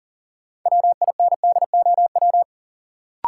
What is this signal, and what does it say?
Morse code, heard as a keyed tone, about 40 wpm, 700 Hz. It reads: WINDOW
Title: Sentences from Top 1600 Words - New Word Drill - 40wpm